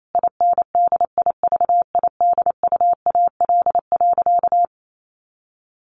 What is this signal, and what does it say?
Morse code, heard as a keyed tone, about 28 wpm, 700 Hz. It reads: INDIVIDUAL.
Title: Sentences from Top 1400 Words 28wpm